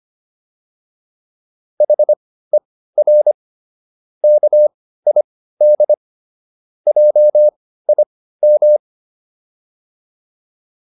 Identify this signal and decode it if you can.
Morse code, heard as a keyed tone, about 25 wpm, spaced as for 15 wpm, 600 Hz. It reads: HER KID JIM